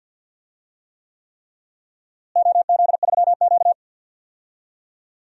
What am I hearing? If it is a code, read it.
Morse code, heard as a keyed tone, about 50 wpm, 700 Hz. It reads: OZ3Q
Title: DXCC-100 Call Signs - 4 Characters - Repeated 3x - 50wpm